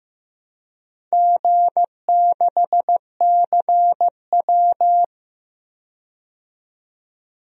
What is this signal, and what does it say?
Morse code, heard as a keyed tone, about 15 wpm, 700 Hz. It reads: G6CW
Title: DXCC-100 Call Signs - 4 Characters - 15wpm